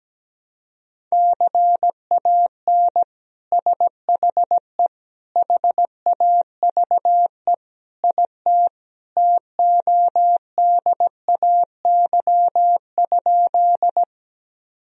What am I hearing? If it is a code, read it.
Morse code, heard as a keyed tone, about 17 wpm, 700 Hz. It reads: CAN SHE HAVE IT TODAY?
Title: Sentences from Top 1000 Words - Review Part 4 - 17wpm